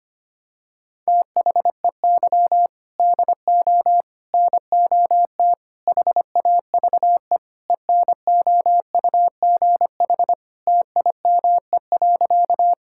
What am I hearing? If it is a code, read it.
Morse code, heard as a keyed tone, about 25 wpm, 700 Hz. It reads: THEY DO NOT HAVE ENOUGH TIME.
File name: practice / 25wpm / Sentences with 4-Word Phrases - 25wpm